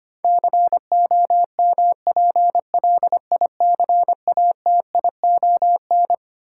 Morse code, heard as a keyed tone, about 25 wpm, 700 Hz. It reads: COMPLICATION
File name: Twelve Letter Words - 25wpm